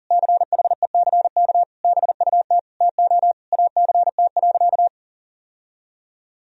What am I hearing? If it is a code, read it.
Morse code, heard as a keyed tone, about 40 wpm, 700 Hz. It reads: CHECK BUT TO ACT.
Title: Sentences from Top 1000 Words - Review Part 3 - 40wpm